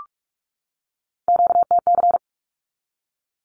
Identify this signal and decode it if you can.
Morse code, heard as a keyed tone, about 44 wpm, 700 Hz. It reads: QTC
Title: Q-Codes 45wpm